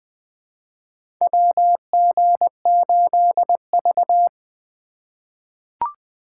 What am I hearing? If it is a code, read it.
Morse code, heard as a keyed tone, about 20 wpm, 700 Hz. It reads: WG8V
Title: USA Call Signs - 4x Character Spacing - 20wpm